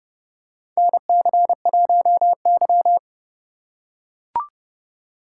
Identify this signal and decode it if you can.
Morse code, heard as a keyed tone, about 30 wpm, 700 Hz. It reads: NC1Y
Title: USA Call Signs - 4 Characters - 2x Character Spacing - 30wpm